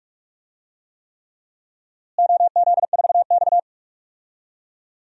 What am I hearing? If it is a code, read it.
Morse code, heard as a keyed tone, about 46 wpm, 700 Hz. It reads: OZ4X
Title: DXCC-100 Call Signs - 4 Characters - Repeated 3x - 45wpm